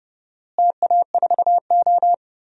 Morse code, heard as a keyed tone, about 32 wpm, 700 Hz. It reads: TA4O
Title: DXCC-100 Call Signs - 4 Characters - Repeated 3x - 30wpm